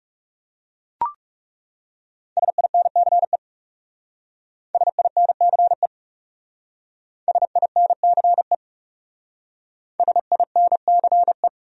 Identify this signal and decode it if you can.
Morse code, heard as a keyed tone, about 38 wpm, 700 Hz. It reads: SINCE SINCE SINCE SINCE
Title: Speed-Racing Top 500 Words 30wpm